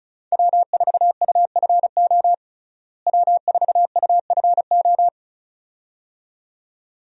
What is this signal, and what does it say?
Morse code, heard as a keyed tone, about 36 wpm, 700 Hz. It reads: W4UFO W4UFO
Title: USA Call Signs - 5 Characters - Repeated 2x - 35wpm